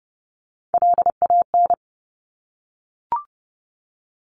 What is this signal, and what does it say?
Morse code, heard as a keyed tone, about 30 wpm, 700 Hz. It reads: LAN
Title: Top Three Letter Combinations in Words 30wpm